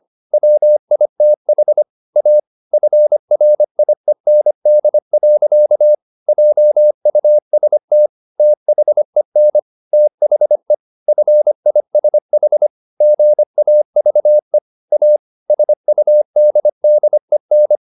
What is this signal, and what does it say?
Morse code, heard as a keyed tone, about 25 wpm, 600 Hz. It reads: WITH A FRIEND. JUST THEN THE FISH GAVE A SUDDEN
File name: the old man and the sea part 1 25wpm